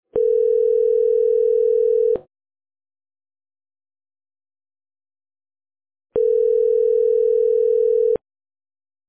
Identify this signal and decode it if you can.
Ringback tone (North America), 440 + 480 Hz.